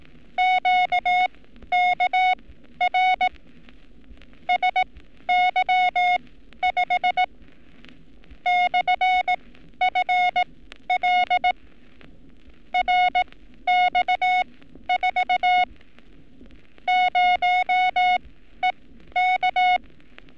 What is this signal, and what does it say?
Morse code, heard as a keyed tone, about 18 wpm, spaced as for 12 wpm, 730 Hz. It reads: QKR SY5 /FL RX4 0EK
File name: training / NAVPERS 11224-11235 / NAVPERS 11227 RA RB